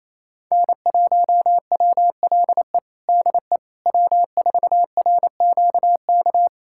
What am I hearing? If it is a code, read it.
Morse code, heard as a keyed tone, about 28 wpm, 700 Hz. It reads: N1WLE DE W4RQK
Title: USA Callsign DE Callsign - 5 Characters - 28wpm